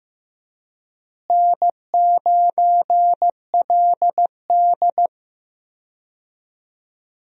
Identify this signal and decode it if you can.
Morse code, heard as a keyed tone, about 15 wpm, 700 Hz. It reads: N9LD
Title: USA Call Signs - 4 Characters - Repeated 3x - 15wpm